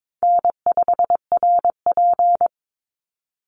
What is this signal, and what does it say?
Morse code, heard as a keyed tone, about 22 wpm, 700 Hz. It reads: N5RP
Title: Call Signs - 22wpm